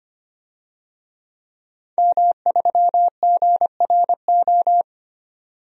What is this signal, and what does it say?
Morse code, heard as a keyed tone, about 25 wpm, 700 Hz. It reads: M3GRO
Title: DXCC-100 Call Signs - 5 Characters - 25wpm